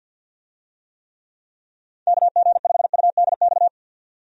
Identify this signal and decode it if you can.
Morse code, heard as a keyed tone, about 52 wpm, 700 Hz. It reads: KG5UDX
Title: USA Call Signs - 6 Characters - 50wpm